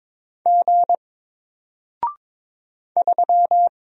Morse code, heard as a keyed tone, about 22 wpm, 700 Hz. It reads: G 3
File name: Single-Letter-Number-22wpm